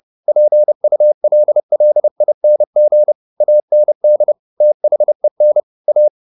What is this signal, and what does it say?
Morse code, heard as a keyed tone, about 30 wpm, 600 Hz. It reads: PULLING AND THEN A
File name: the old man and the sea / the old man and the sea 1 30wpm